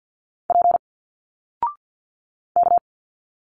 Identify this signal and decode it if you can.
Morse code, heard as a keyed tone, about 50 wpm, 700 Hz. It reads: P K
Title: Single Letters 50wpm